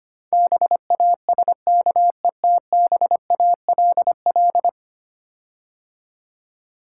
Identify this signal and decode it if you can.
Morse code, heard as a keyed tone, about 25 wpm, 700 Hz. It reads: BASKETBALL
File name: 10 Letter Words - First 3 Letters Given - 25wpm